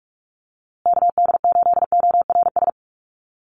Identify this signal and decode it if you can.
Morse code, heard as a keyed tone, about 44 wpm, 700 Hz. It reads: KD8ORS
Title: USA Call Signs Repeated 2x 45wpm